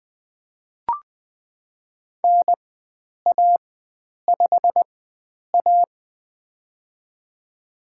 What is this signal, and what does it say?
Morse code, heard as a keyed tone, about 20 wpm, spaced as for 9 wpm, 700 Hz. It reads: NA5A
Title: USA Call Signs - 4x Character Spacing - 20wpm